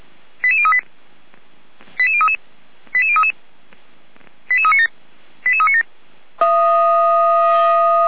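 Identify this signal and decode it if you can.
Five-tone selective call (ZVEI), heard as quick runs of five tones, tones of about 70 ms. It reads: 88038 88033 88033 88397 88397
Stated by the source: ZVEI five-tone call received by SDR, logged on Feuerwehr Orlamünde audio server.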